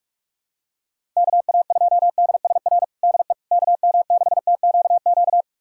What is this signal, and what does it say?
Morse code, heard as a keyed tone, about 41 wpm, 700 Hz. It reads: KA2DSR DE KM6TQQ
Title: USA Callsign DE Callsign 45wpm